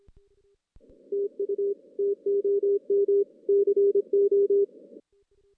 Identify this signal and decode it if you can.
Morse code, heard as a keyed tone, about 23 wpm, 400 Hz. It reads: TU TOM CO
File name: 2016 JAN CW NAQP / 2016 Jan naqp k5ot - 0-6